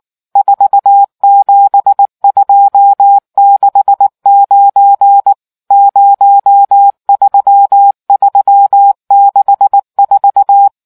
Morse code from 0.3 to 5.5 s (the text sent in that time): 47269